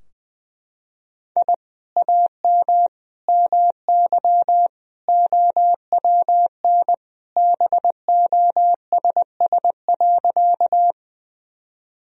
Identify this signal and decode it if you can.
Morse code, heard as a keyed tone, about 21 wpm, 700 Hz. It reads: I AM MY OWN BOSS.
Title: Sentences from Top 1000 Words - Review Part 4 - 20wpm